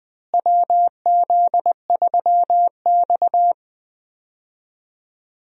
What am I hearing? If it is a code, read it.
Morse code, heard as a keyed tone, about 20 wpm, 700 Hz. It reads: WZ3X
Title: USA Call Signs - 4 Characters - Repeated 2x - 20wpm